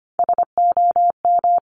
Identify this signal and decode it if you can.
Morse code, heard as a keyed tone, about 25 wpm, 700 Hz. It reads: SOM
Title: Top Three Letter Combinations in Words 25wpm